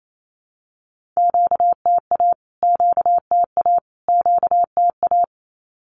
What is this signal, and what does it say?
Morse code, heard as a keyed tone, about 28 wpm, 700 Hz. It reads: QTA QTA QTA
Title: Q-Codes Repeated 3x 28wpm